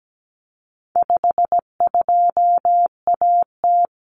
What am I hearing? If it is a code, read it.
Morse code, heard as a keyed tone, about 17 wpm, 700 Hz. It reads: 52AT
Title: Common Cut Numbers - 4-Digit 17wpm